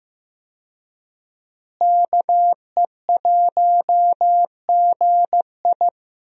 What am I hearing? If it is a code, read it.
Morse code, heard as a keyed tone, about 15 wpm, 700 Hz. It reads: KE1GI
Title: USA Call Signs - 5 Characters - Repeated 2x - 15wpm